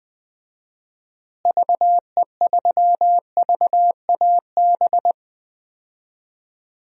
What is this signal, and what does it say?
Morse code, heard as a keyed tone, about 21 wpm, 700 Hz. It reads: VE3VAB
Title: DXCC-100 Call Signs - 6 Characters - Repeated 3x - 20wpm